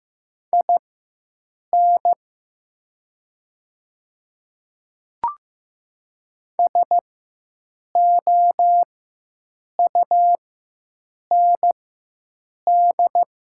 Morse code, heard as a keyed tone, about 15 wpm, spaced as for 7 wpm, 700 Hz. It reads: IN SOUND
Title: 4x Character Spacing - Top 300 Words - 15wpm